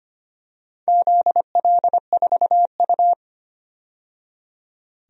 Morse code, heard as a keyed tone, about 25 wpm, 700 Hz. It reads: ZL4U
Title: DXCC-100 Call Signs - 4 Characters - 25wpm